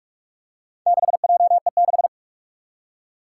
Morse code, heard as a keyed tone, about 41 wpm, 700 Hz. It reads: 6JE6
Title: Four Challenging Letter-Number Combinations 45wpm